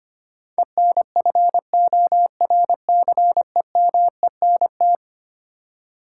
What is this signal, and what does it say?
Morse code, heard as a keyed tone, about 25 wpm, 700 Hz. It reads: ENFORCEMENT
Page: morse-code-ninja-practice-25wpm